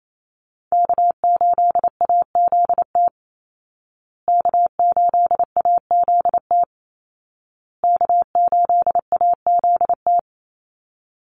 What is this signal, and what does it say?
Morse code, heard as a keyed tone, about 28 wpm, 700 Hz. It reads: K8AZT K8AZT K8AZT
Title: Call Signs Repeated 2x - 28wpm